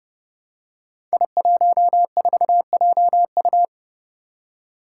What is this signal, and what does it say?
Morse code, heard as a keyed tone, about 30 wpm, 700 Hz. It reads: I14JU
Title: Five Challenging Letter-Number Combinations 30wpm